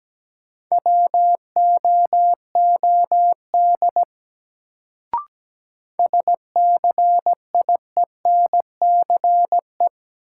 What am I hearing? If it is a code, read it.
Morse code, heard as a keyed tone, about 17 wpm, 700 Hz. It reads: WOOD SCIENCE
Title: Sentences from Top 1000 Words - Review Part 2 - New Word Drill - 17wpm